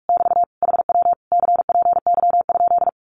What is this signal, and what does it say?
Morse code, heard as a keyed tone, about 46 wpm, 700 Hz. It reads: <BT> HW CPY?
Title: Common QSO Phrases Encore 45wpm